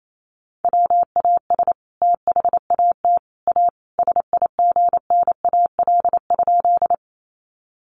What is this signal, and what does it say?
Morse code, heard as a keyed tone, about 28 wpm, 700 Hz. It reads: WAS THAT A SIGNAL?
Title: Sentences from Top 1400 Words 28wpm